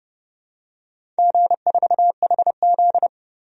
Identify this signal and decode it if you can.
Morse code, heard as a keyed tone, about 30 wpm, 700 Hz. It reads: G4HZ
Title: DXCC-100 Call Signs - 4 Characters - 30wpm